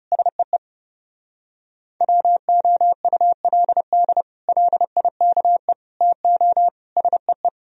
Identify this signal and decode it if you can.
Morse code, heard as a keyed tone, about 36 wpm, 690 Hz. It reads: SEE WOULD LIKE TO SEE
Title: Speed-Racing - Sets of 4 Words 30wpm